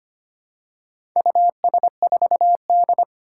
Morse code, heard as a keyed tone, about 25 wpm, 700 Hz. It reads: US4D
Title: Four Challenging Letter-Number Combinations 25wpm